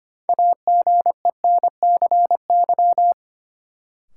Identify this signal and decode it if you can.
Morse code, heard as a keyed tone, about 25 wpm, 700 Hz. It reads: AGENCY
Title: Sentences from Top 1300 Words - New Word Drill Alt - 25wpm